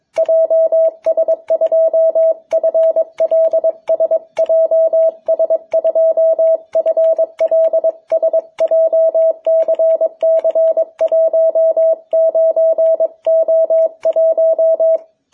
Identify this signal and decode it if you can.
Morse code, heard as a keyed tone, about 22 wpm, 660 Hz. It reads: JS2FLS JS2FLS JCC19O1
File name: jh8cbh / 2021cw